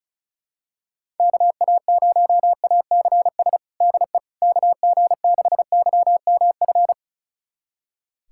Morse code, heard as a keyed tone, about 34 wpm, 700 Hz. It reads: KA0ACS DE KG6YMF
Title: USA Callsign DE Callsign 35wpm